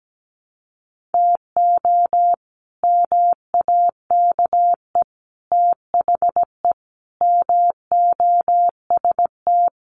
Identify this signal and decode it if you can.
Morse code, heard as a keyed tone, about 17 wpm, 700 Hz. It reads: TO MAKE THE MOST
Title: Sets of 5 Words Encore2 17wpm